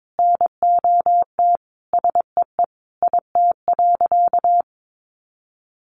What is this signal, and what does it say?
Morse code, heard as a keyed tone, about 22 wpm, 700 Hz. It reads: NOT SEE IT.
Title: Sentences from Top 1000 Words 22wpm